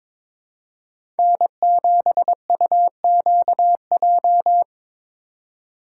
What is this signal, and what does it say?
Morse code, heard as a keyed tone, about 26 wpm, 700 Hz. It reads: N7UQJ
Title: Speed Racing - USA Call Signs 22wpm